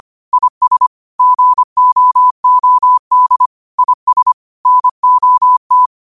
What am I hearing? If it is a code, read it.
Morse code, heard as a keyed tone, about 25 wpm, 1000 Hz. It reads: IS GOOD IS NOT
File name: 25wpm